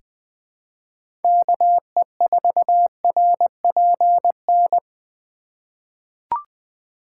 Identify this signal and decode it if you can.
Morse code, heard as a keyed tone, about 20 wpm, 700 Hz. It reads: KE4RPN
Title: USA Call Signs - 6 Characters - 3x Character Spacing - 20wpm